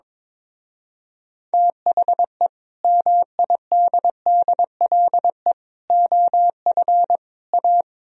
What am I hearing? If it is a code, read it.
Morse code, heard as a keyed tone, about 23 wpm, 700 Hz. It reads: THE MIDDLE OF A
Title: Speed-Racing - Sets of 4 Words 22wpm